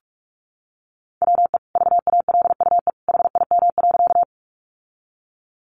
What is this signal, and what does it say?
Morse code, heard as a keyed tone, about 44 wpm, 700 Hz. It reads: WE VALUE HIM.
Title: Sentences from Top 600 Words 45wpm